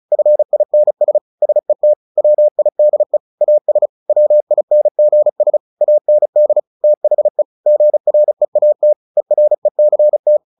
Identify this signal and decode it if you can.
Morse code, heard as a keyed tone, about 36 wpm, 600 Hz. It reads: FINS SET WIDE AS WINGS AND THE GREAT ERECT
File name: the old man and the sea / the old man and the sea 1 35wpm